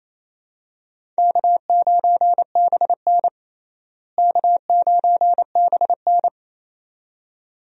Morse code, heard as a keyed tone, about 27 wpm, 700 Hz. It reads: K9BN K9BN
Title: USA Call Signs - 4 Characters - Repeated 2x - 28wpm